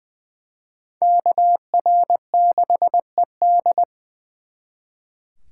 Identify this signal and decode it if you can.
Morse code, heard as a keyed tone, about 24 wpm, 700 Hz. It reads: KR6ED
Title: Speed Racing - USA 2x2 Call Signs 20wpm